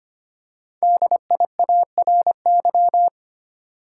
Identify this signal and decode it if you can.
Morse code, heard as a keyed tone, about 25 wpm, 700 Hz. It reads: DIARY
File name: 5 Letter Words - First 3 Letters Given - 25wpm